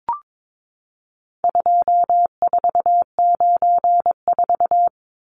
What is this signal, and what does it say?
Morse code, heard as a keyed tone, about 22 wpm, 700 Hz. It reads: 2494
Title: Numbers - 4-Digit 22wpm